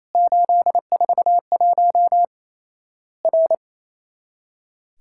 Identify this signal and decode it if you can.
Morse code, heard as a keyed tone, about 28 wpm, 700 Hz. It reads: 841 R
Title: Worked All Europe DX Contest - QTC Exchange - 28wpm